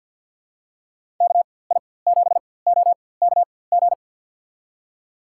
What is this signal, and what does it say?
Morse code, heard as a keyed tone, about 51 wpm, spaced as for 23 wpm, 700 Hz. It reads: KI7OKG
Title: USA Call Signs - 6 Characters - 4x Character Spacing - 50wpm